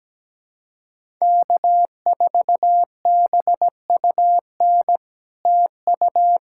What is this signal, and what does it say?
Morse code, heard as a keyed tone, about 17 wpm, 700 Hz. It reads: K4BUN TU
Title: SOTA QSO Practice 17wpm